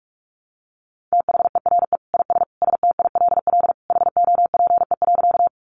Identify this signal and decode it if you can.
Morse code, heard as a keyed tone, about 44 wpm, 700 Hz. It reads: THERE IS STILL HOPE.
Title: Sentences from Top 1000 Words 45wpm